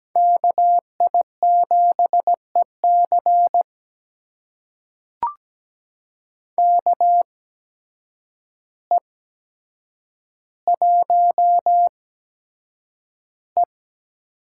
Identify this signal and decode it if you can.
Morse code, heard as a keyed tone, about 17 wpm, spaced as for 8 wpm, 700 Hz. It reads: KI7EC KE1E
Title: USA Call Signs - 5 Characters - 8x Character Spacing - 17wpm